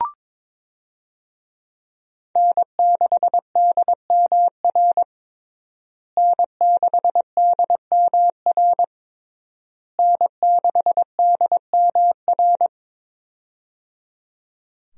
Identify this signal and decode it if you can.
Morse code, heard as a keyed tone, about 22 wpm, 700 Hz. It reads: N6DMR N6DMR N6DMR
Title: USA Call Signs - 5 Characters - Repeated 3x - 22wpm